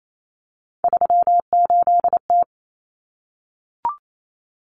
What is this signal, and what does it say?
Morse code, heard as a keyed tone, about 28 wpm, 700 Hz. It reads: 38T
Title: Common Cut Numbers - 3-Digit 28wpm